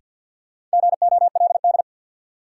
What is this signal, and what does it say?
Morse code, heard as a keyed tone, about 53 wpm, 700 Hz. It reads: GOLD
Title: Sentences from Top 1100 Words - New Word Drill Alt - 50wpm